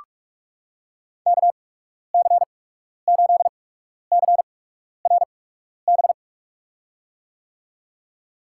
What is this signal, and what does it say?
Morse code, heard as a keyed tone, about 46 wpm, spaced as for 12 wpm, 690 Hz. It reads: KC8CRB
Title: USA Call Signs - 8x Character Spacing - 45wpm